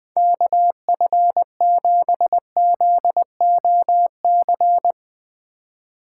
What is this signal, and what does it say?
Morse code, heard as a keyed tone, about 21 wpm, 700 Hz. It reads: KF7ZOC